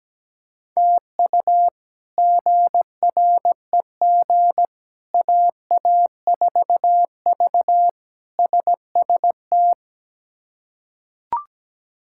Morse code, heard as a keyed tone, about 17 wpm, 700 Hz. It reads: TU GREG AA4V SST
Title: K1USN SST QSO Practice - 17wpm